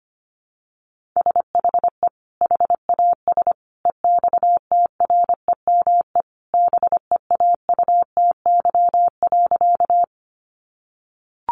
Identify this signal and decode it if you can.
Morse code, heard as a keyed tone, about 25 wpm, 700 Hz. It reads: SHE HAS EXTREME BEAUTY.
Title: Sentences from Top 1300 Words 25wpm